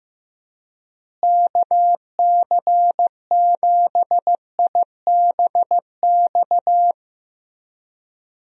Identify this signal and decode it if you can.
Morse code, heard as a keyed tone, about 15 wpm, 700 Hz. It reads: KC7IBX
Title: USA Call Signs - 6 Characters - Repeated 3x - 15wpm